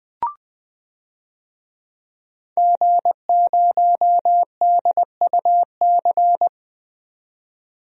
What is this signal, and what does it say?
Morse code, heard as a keyed tone, about 21 wpm, 700 Hz. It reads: G0DUC
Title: DXCC-100 Call Signs - 5 Characters - 20wpm